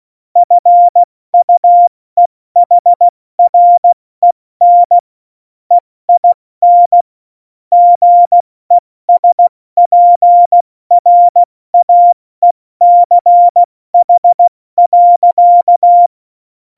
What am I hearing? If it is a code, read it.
Morse code, heard as a keyed tone, about 16 wpm, 700 Hz. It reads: FUEHREN EIN GESPRAECH.
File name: Morsekurs / Witze / Witz 18. Juni 2019, 70 F80